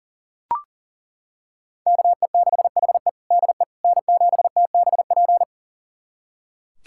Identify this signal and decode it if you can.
Morse code, heard as a keyed tone, about 40 wpm, 700 Hz. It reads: KE6HE DE N7TBP
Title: USA Callsign DE Callsign - 5 Characters - 40wpm